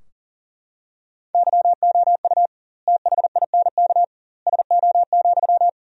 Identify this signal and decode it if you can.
Morse code, heard as a keyed tone, about 40 wpm, 700 Hz. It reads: YOU THINK SO,
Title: 2x Word-Spacing - Sentences from Top 200 Words - 40wpm